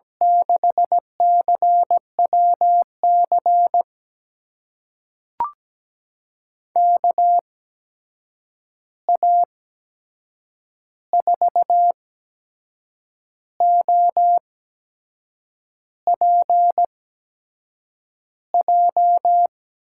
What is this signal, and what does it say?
Morse code, heard as a keyed tone, about 17 wpm, spaced as for 6 wpm, 700 Hz. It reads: 6CWC KA4OPJ
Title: USA Call Signs - 6 Characters - 8x Character Spacing - 17wpm